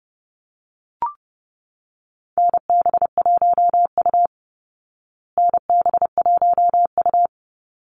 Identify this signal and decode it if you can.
Morse code, heard as a keyed tone, about 30 wpm, 700 Hz. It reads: NB1U NB1U
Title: Call Signs Repeated 3x - 30wpm